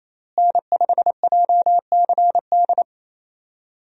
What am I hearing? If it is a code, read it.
Morse code, heard as a keyed tone, about 28 wpm, 700 Hz. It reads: N5JCD
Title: USA Call Signs - 5 Characters - Repeated 2x - 28wpm